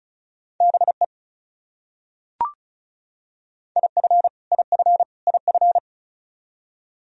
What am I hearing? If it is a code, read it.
Morse code, heard as a keyed tone, about 34 wpm, 700 Hz. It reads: BE IF IF IF
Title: Top 100 Words Repeated 35wpm